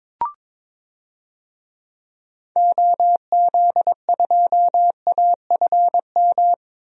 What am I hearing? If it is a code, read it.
Morse code, heard as a keyed tone, about 22 wpm, 700 Hz. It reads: OZ2AFM
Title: DXCC-100 Call Signs - 6 Characters - Repeated 2x - 22wpm